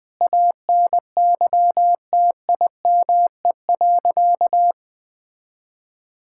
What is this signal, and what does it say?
Morse code, heard as a keyed tone, about 20 wpm, 700 Hz. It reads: ANYTIME.
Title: Difficult Sentences with Hints - 20wpm